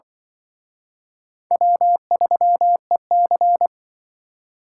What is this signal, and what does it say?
Morse code, heard as a keyed tone, about 24 wpm, 700 Hz. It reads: W3EC
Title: Speed Racing - USA 1x2 Call Signs 20wpm